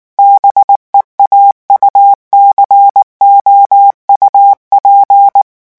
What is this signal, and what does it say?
Morse code, heard as a keyed tone, about 19 wpm, 800 Hz. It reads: BEAUCOUP
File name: morse mp3 / revision9 citations humoristiques vitesse 18 mots minute